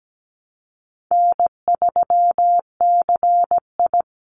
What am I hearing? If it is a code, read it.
Morse code, heard as a keyed tone, about 17 wpm, 700 Hz. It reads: N3CI
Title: Call Signs - 17wpm